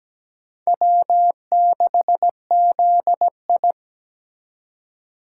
Morse code, heard as a keyed tone, about 17 wpm, 700 Hz. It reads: W6ZI
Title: USA Call Signs - 4 Characters - 4x Character Spacing - 17wpm